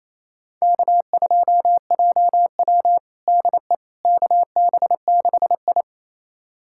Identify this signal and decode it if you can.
Morse code, heard as a keyed tone, about 28 wpm, 700 Hz. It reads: K2JW DE KB6I
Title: USA Callsign DE Callsign - 4 Characters - 28wpm